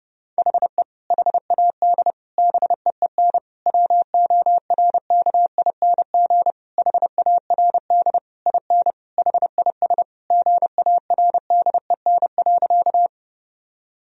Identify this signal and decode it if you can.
Morse code, heard as a keyed tone, about 30 wpm, 700 Hz. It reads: HE HAD BEEN WORKING HARD IN HIS GARDEN.
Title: Sentences from Top 1000 Words - Review Part 4 - 30wpm